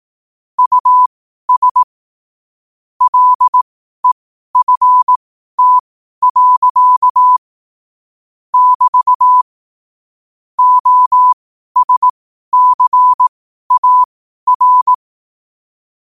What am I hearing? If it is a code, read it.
Morse code, heard as a keyed tone, about 18 wpm, spaced as for 12 wpm, 1000 Hz. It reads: US LEFT. <BT> OSCAR
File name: qotdfiles / 12wpm